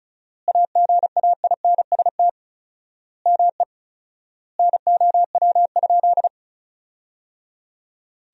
Morse code, heard as a keyed tone, about 36 wpm, 700 Hz. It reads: AGAINST ME NOW?
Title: 4x Word-Spacing - Sentences from Top 400 Words - 35wpm